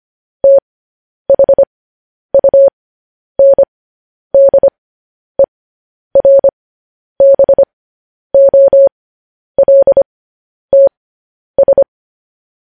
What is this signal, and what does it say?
Morse code, heard as a keyed tone, about 25 wpm, spaced as for 10 wpm, 550 Hz. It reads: THUNDERBOLTS
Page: news headlines in morse code at 10 wpm